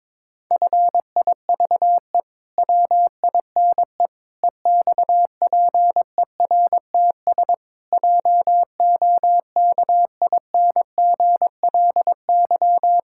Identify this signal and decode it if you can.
Morse code, heard as a keyed tone, about 22 wpm, 700 Hz. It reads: FIVE WINE EXPERTS JOKINGLY
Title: Pangram Challenge v2 - 22wpm